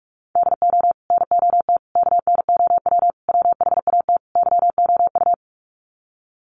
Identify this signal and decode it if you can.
Morse code, heard as a keyed tone, about 44 wpm, 700 Hz. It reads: DO NOT KNOW WHAT YOU
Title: Sets of 5 Words 45wpm